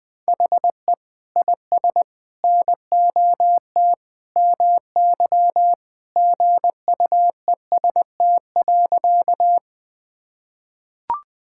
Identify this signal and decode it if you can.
Morse code, heard as a keyed tone, about 20 wpm, 700 Hz. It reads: HE IS NOT MY GUEST.
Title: Sentences from Top 1600 Words 20wpm